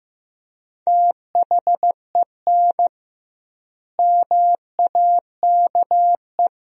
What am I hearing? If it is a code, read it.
Morse code, heard as a keyed tone, about 15 wpm, 700 Hz. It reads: THEN MAKE
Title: 2x Word-Spacing - Sentences from Top 100 Words - 15wpm